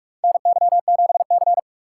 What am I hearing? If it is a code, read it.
Morse code, heard as a keyed tone, about 44 wpm, 690 Hz. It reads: NY7C
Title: USA Call Signs - 4x Character Spacing - 45wpm